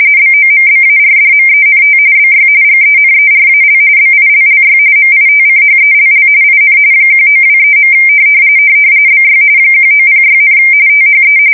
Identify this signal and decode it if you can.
RTTY, about 45.45 baud. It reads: BOMBARDMENT BY GENERAL BILLY MITCHELL IN 1923. BY 1906, THE
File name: fsk170 / battleshipnj